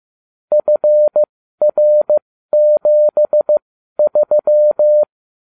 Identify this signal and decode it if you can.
Morse code, heard as a keyed tone, about 15 wpm, spaced as for 12 wpm, 600 Hz. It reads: FR73
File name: cw / ubacw34 3